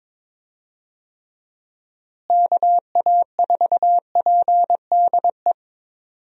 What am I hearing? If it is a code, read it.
Morse code, heard as a keyed tone, about 22 wpm, 700 Hz. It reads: KA4PDE
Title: USA Call Signs - 6 Characters - Repeated 2x - 22wpm